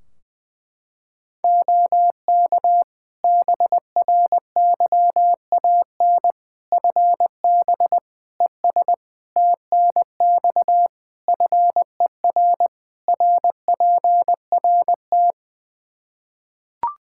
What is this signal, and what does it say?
Morse code, heard as a keyed tone, about 21 wpm, 700 Hz. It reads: OK BRYAN FB ES TNX FER RPRT
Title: Practice QSO - Exchange 2 - Info - 20wpm